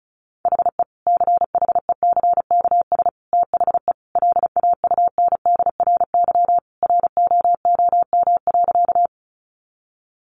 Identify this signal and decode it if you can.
Morse code, heard as a keyed tone, about 34 wpm, 700 Hz. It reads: HE CHECKS THE LAUNDRY ROOM.
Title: Easy Phrases 35wpm